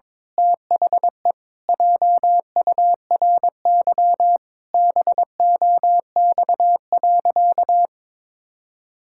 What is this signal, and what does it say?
Morse code, heard as a keyed tone, about 22 wpm, 700 Hz. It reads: THE JURY BOX.
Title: Pangram Challenge v2 - 22wpm